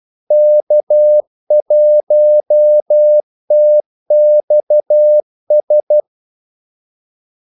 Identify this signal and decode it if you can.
Morse code, heard as a keyed tone, about 12 wpm, 600 Hz. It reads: K1TXS
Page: 2 ekstra ordmellemrum mellem kaldesignalerne